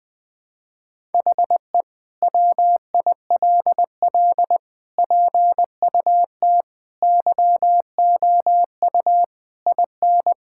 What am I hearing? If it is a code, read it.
Morse code, heard as a keyed tone, about 21 wpm, 700 Hz. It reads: HE WILL PUT YOU IN
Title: Sentences from Top 1000 Words - Review Part 5 - 20wpm